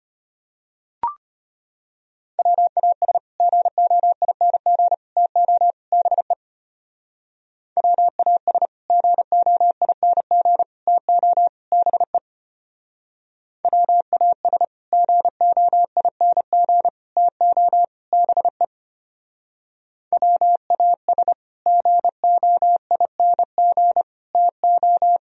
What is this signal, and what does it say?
Morse code, heard as a keyed tone, about 32 wpm, 700 Hz. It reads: WAS GOING TO BE WAS GOING TO BE WAS GOING TO BE WAS GOING TO